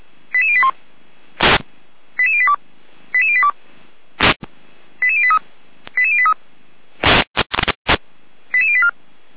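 Five-tone selective call (ZVEI), heard as quick runs of five tones, tones of about 70 ms. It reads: 88081 88082 88082 88083 88083 88084
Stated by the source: ZVEI five-tone selective call received by SDR.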